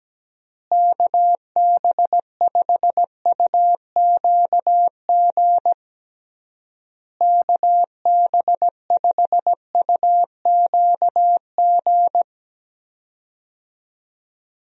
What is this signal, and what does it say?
Morse code, heard as a keyed tone, about 17 wpm, 700 Hz. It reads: KB5UQG KB5UQG
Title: USA Call Signs - 6 Characters - Repeated 3x - 17wpm